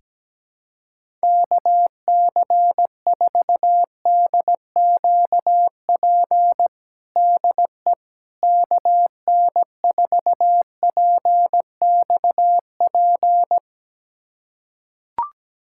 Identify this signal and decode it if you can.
Morse code, heard as a keyed tone, about 17 wpm, 700 Hz. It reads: KC4DQP DE KN4PXP
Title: USA Callsign DE Callsign 17wpm